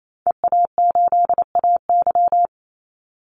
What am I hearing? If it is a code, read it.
Morse code, heard as a keyed tone, about 28 wpm, 700 Hz. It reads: EA8AY
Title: Call Signs - 28wpm